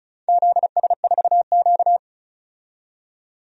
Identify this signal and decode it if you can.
Morse code, heard as a keyed tone, about 36 wpm, 700 Hz. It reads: ZS4Q
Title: DXCC-100 Call Signs - 4 Characters - 35wpm